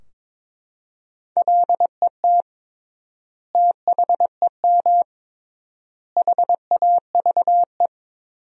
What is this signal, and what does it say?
Morse code, heard as a keyed tone, about 22 wpm, 700 Hz. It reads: LET THEM HAVE
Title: 3x Word-Spacing - Sentences from Top 300 Words - 22wpm